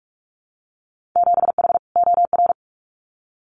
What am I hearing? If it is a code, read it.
Morse code, heard as a keyed tone, about 44 wpm, 700 Hz. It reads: 7H OR
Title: Winter Field Day Exchanges Encore 45wpm